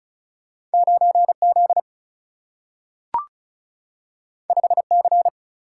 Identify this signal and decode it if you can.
Morse code, heard as a keyed tone, about 36 wpm, 700 Hz. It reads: 9Z 5C
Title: Single-Letter Single-Number 35wpm